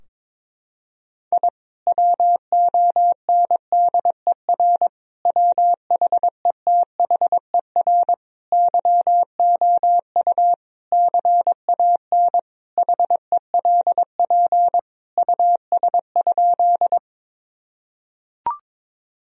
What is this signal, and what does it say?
Morse code, heard as a keyed tone, about 22 wpm, 700 Hz. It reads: I WONDER WHETHER YOU CAN HELP US?
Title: Sentences from Top 1000 Words - Review Part 3 - 22wpm